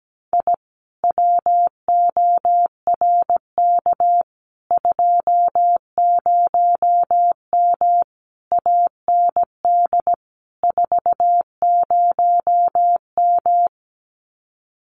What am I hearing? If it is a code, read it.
Morse code, heard as a keyed tone, about 17 wpm, 700 Hz. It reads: I WORK 20M AND 40M
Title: Common QSO Phrases Encore 17wpm